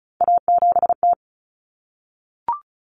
Morse code, heard as a keyed tone, about 34 wpm, 700 Hz. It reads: A7T
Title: Common Cut Numbers - 3-Digit 35wpm